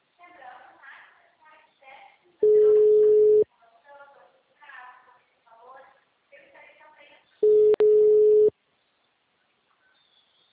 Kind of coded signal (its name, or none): telephone call progress tone